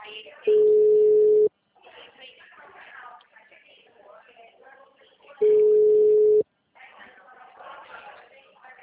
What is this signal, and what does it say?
Ringback tone (Europe, 425 Hz), with speech.
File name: out- -2014- - - .17